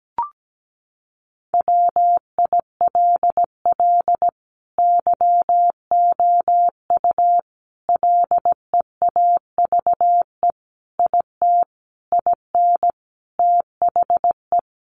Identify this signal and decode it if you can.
Morse code, heard as a keyed tone, about 17 wpm, 700 Hz. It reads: WILL YOU LEAVE IT IN THE
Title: Sentences from Top 1300 Words 17wpm